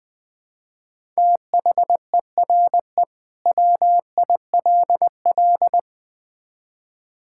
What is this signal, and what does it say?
Morse code, heard as a keyed tone, about 20 wpm, 700 Hz. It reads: THERE WILL
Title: Build A Phrase - 4-Word Phrases - 20wpm